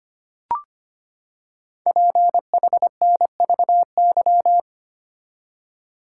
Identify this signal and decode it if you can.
Morse code, heard as a keyed tone, about 25 wpm, 700 Hz. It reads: PHNVY
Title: Code Groups - 5 Letters - 25wpm